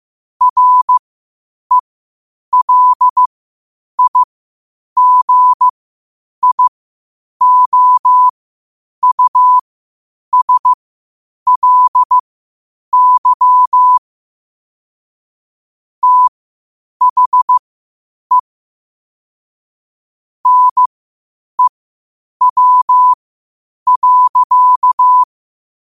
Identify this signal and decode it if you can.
Morse code, heard as a keyed tone, about 15 wpm, spaced as for 8 wpm, 1000 Hz. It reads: RELIGIOUSLY THE NEW.
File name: qotdfiles / 8wpm